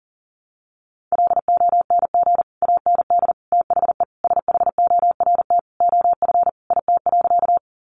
Morse code, heard as a keyed tone, about 40 wpm, 700 Hz. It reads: LONG AND THE SHORT OF IT.
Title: Sentences from Top 400 Words 40wpm